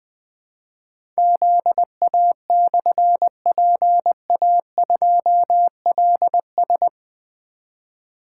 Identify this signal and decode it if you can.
Morse code, heard as a keyed tone, about 20 wpm, 700 Hz. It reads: ZA/PA2LS
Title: Difficult Call Signs 20wpm